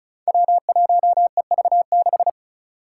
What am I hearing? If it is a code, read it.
Morse code, heard as a keyed tone, about 34 wpm, 700 Hz. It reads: W1EV6
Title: Five Challenging Letter-Number Combinations 35wpm